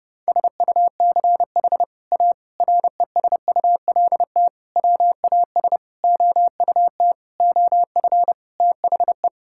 Morse code, heard as a keyed tone, about 30 wpm, 700 Hz. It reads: SUCH A RESULT WAS OUT OF THE
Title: Sentences from Top 1000 Words - Review Part 5 - 30wpm